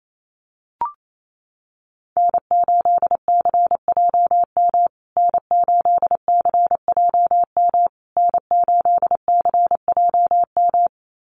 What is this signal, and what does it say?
Morse code, heard as a keyed tone, about 28 wpm, 700 Hz. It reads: N8CJM N8CJM N8CJM
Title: USA Call Signs Repeated 3x 28wpm